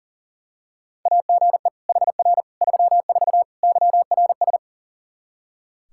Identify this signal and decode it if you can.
Morse code, heard as a keyed tone, about 37 wpm, 700 Hz. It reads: AGE HR 34 YRS
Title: Practice QSO - Exchange 3 - More Info - 40wpm